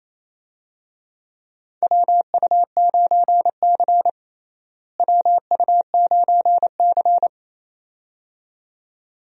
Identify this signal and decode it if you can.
Morse code, heard as a keyed tone, about 28 wpm, 700 Hz. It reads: WU9C WU9C
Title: USA Call Signs - 4 Characters - Repeated 2x - 28wpm